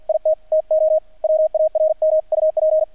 Morse code, heard as a keyed tone, about 48 wpm, 630 Hz, with radio noise. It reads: IT TO WARMUP
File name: archived / CRSnet